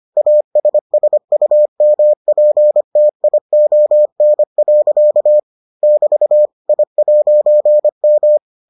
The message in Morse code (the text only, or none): ASSUMPTION. <BT> I'M